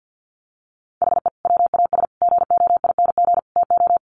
Morse code, heard as a keyed tone, about 50 wpm, 700 Hz. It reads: HE WAS GOING TO
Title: Sets of 4 Words 50wpm